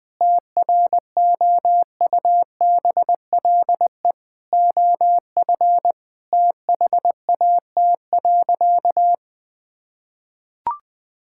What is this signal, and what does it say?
Morse code, heard as a keyed tone, about 20 wpm, 700 Hz. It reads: TROUBLE OF THAT.
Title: Sentences from Top 1600 Words 20wpm